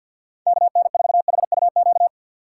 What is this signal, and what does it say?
Morse code, heard as a keyed tone, about 52 wpm, 700 Hz. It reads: KN4HUQ
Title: USA Call Signs - 2x Character Spacing - 50wpm